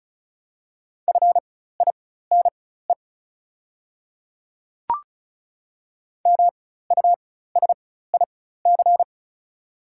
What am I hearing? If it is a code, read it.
Morse code, heard as a keyed tone, about 36 wpm, spaced as for 16 wpm, 700 Hz. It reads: FINE MUSIC
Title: 4x Character Spacing - Top 500 Words - 35wpm